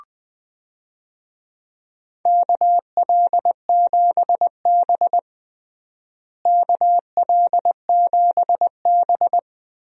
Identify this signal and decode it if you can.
Morse code, heard as a keyed tone, about 20 wpm, 700 Hz. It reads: KL7B KL7B
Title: DXCC-100 Call Signs - 4 Characters - Repeated 3x - 20wpm